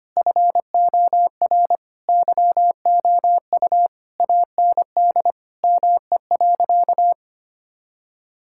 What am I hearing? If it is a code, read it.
Morse code, heard as a keyed tone, about 25 wpm, 700 Hz. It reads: FOR YOU AND ME.
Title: Sentences from Top 1600 Words 25wpm